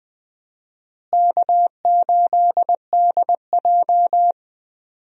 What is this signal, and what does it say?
Morse code, heard as a keyed tone, about 20 wpm, 700 Hz. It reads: K8DJ
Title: USA Call Signs - 4 Characters - Repeated 2x - 20wpm